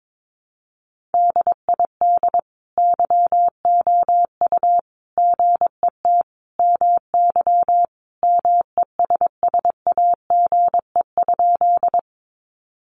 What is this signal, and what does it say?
Morse code, heard as a keyed tone, about 22 wpm, 700 Hz. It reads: DID YOU GET MY MESSAGE?
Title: Sentences from Top 1100 Words 22wpm